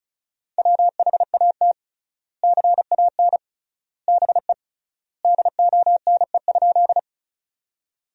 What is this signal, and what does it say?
Morse code, heard as a keyed tone, about 36 wpm, 700 Hz. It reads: WHAT CAN BE DONE?
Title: 3x Word-Spacing - Sentences from Top 500 Words - 35wpm